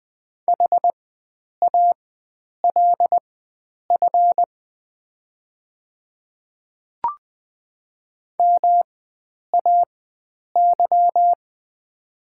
Morse code, heard as a keyed tone, about 20 wpm, spaced as for 9 wpm, 700 Hz. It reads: HALF MAY
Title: 4x Character Spacing - Top 500 Words - 20wpm